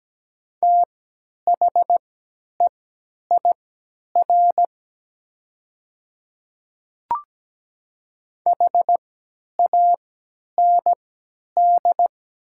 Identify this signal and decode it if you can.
Morse code, heard as a keyed tone, about 17 wpm, spaced as for 10 wpm, 700 Hz. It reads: THEIR HAND
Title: 3x Character Spacing - Top 300 Words - 17wpm